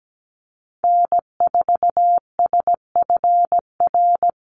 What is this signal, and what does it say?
Morse code, heard as a keyed tone, about 17 wpm, 700 Hz. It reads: N4SFR
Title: NA SOTA Activator Callsigns 17wpm